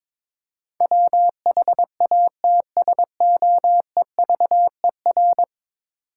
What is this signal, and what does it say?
Morse code, heard as a keyed tone, about 22 wpm, 700 Hz. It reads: WHATSOEVER
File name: practice / 22wpm / 10 Letter Words - First 3 Letters Given - 22wpm